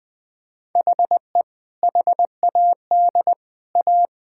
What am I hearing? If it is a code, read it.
Morse code, heard as a keyed tone, about 21 wpm, 700 Hz. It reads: HE HAD A
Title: Sentences from Top 1000 Words - Review Part 6 - 20wpm